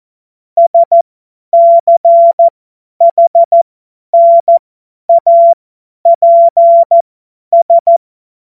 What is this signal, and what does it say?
Morse code, heard as a keyed tone, about 14 wpm, spaced as for 10 wpm, 680 Hz. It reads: SCHNAPS
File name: Morsekurs / Witze / Witz 23 Januar 2024 50 F70